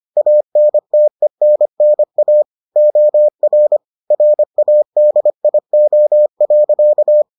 Morse code, heard as a keyed tone, about 25 wpm, 600 Hz. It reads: ANTENNA OR RADIO.